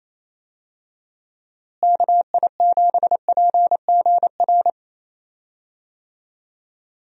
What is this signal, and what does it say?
Morse code, heard as a keyed tone, about 28 wpm, 700 Hz. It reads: KI7PGR
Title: USA Call Signs - 6 Characters - Repeated 3x - 28wpm